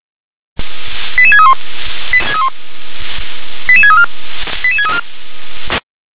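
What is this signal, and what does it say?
Five-tone selective call (ZVEI), heard as quick runs of five tones, tones of about 80 ms. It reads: 88631 88634 88634
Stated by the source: ZVEI five-tone selective call 88150, received by SDR.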